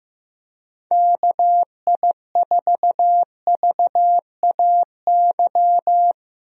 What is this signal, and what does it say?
Morse code, heard as a keyed tone, about 15 wpm, 700 Hz. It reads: KI4VAY
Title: USA Call Signs - 6 Characters - 15wpm